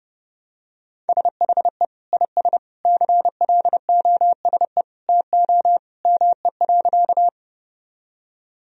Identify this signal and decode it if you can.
Morse code, heard as a keyed tone, about 30 wpm, 700 Hz. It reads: SHE IS CLOSE TO ME.
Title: Sentences from Top 1000 Words - Review Part 1 - 30wpm